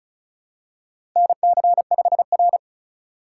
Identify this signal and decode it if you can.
Morse code, heard as a keyed tone, about 36 wpm, 700 Hz. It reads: NC5R
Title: USA Call Signs - 4 Characters - 2x Character Spacing - 35wpm